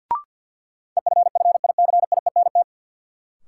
Morse code, heard as a keyed tone, about 50 wpm, 700 Hz. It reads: EFFICIENT
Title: Sentences from Top 1300 Words - New Word Drill - 50wpm